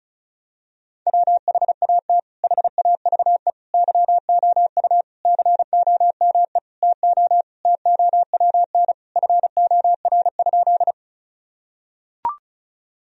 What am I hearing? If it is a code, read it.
Morse code, heard as a keyed tone, about 34 wpm, 700 Hz. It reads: WHAT HAVE YOU COME TO TOWN FOR?
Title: Sentences from Top 1000 Words - Review Part 2 - 35wpm